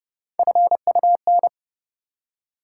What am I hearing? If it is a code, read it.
Morse code, heard as a keyed tone, about 32 wpm, 700 Hz. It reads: FUN
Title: Three Letter Words - 30wpm